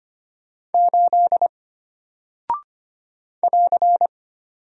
Morse code, heard as a keyed tone, about 25 wpm, 700 Hz. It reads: 8 <AR>
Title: Single Character (Letter:Number:Punctuation:Prosign) Alt 25wpm